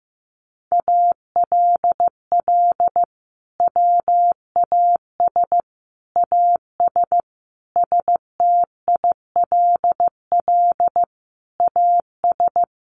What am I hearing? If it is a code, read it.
Morse code, heard as a keyed tone, about 15 wpm, 700 Hz. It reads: ALL WAS AS STILL AS
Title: Sentences from Top 800 Words 15wpm